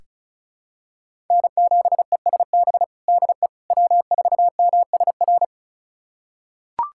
Morse code, heard as a keyed tone, about 36 wpm, 700 Hz. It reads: N7ESB DE W4MSR
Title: USA Callsign DE Callsign - 5 Characters - 35wpm